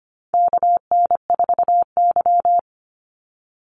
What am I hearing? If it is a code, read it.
Morse code, heard as a keyed tone, about 25 wpm, 700 Hz. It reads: KN4Y
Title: Call Signs Repeated 2x - 25wpm